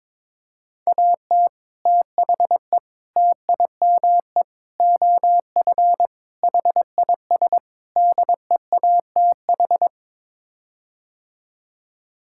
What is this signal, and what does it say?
Morse code, heard as a keyed tone, about 22 wpm, 700 Hz. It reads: AT THE TIME OF HIS DEATH
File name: practice / 22wpm / Sets of 6 Words - 22wpm